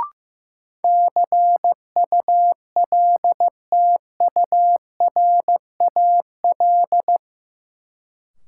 Morse code, heard as a keyed tone, about 15 wpm, 700 Hz. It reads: CULTURAL
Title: Sentences from Top 1500 Words - New Word Drill Alt - 15wpm